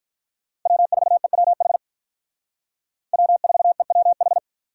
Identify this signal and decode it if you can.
Morse code, heard as a keyed tone, about 43 wpm, 700 Hz. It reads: W4EWH W4EWH
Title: Speed Racing - USA 1x3 Call Signs 35wpm